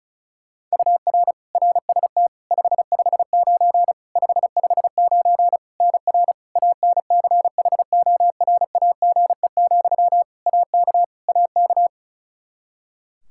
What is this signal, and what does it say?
Morse code, heard as a keyed tone, about 36 wpm, 700 Hz. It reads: UR RST 559 559 NR ANCHORAGE, AK AK
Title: Practice QSO - Exchange 1 - Beginning - 35wpm